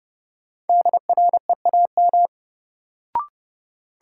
Morse code, heard as a keyed tone, about 30 wpm, 700 Hz. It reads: DREAM
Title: Sentences from Top 1500 Words - New Word Drill - 30wpm